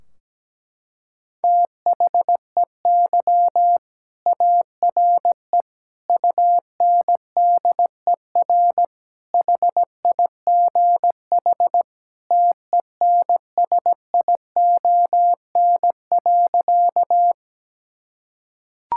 Morse code, heard as a keyed tone, about 17 wpm, 700 Hz. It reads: THEY ARE UNDER HIGH TENSION.
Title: Sentences from Top 1600 Words 17wpm